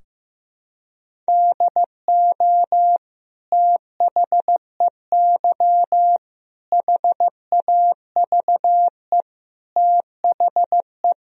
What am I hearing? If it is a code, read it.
Morse code, heard as a keyed tone, about 16 wpm, 700 Hz. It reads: DO THEY HAVE THE
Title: Sentences from Top 1000 Words - Review Part 3 - 15wpm